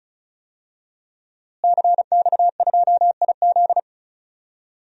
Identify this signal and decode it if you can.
Morse code, heard as a keyed tone, about 36 wpm, 700 Hz. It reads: CX2IZ
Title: DXCC-100 Call Signs - 5 Characters - Repeated 2x - 35wpm